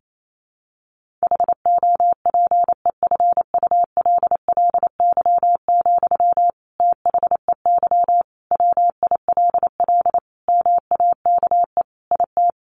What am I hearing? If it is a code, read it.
Morse code, heard as a keyed tone, about 28 wpm, 700 Hz. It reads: HOPEFULLY, THEY WILL MAKE IT
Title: Sentences from Top 1500 Words 28wpm